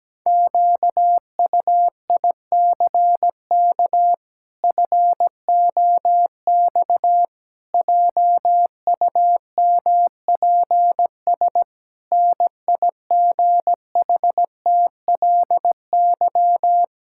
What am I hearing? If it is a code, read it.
Morse code, heard as a keyed tone, about 17 wpm, 700 Hz. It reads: QUICK FOX JUMPS NIGHTLY
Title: Pangram Challenge v2 - 17wpm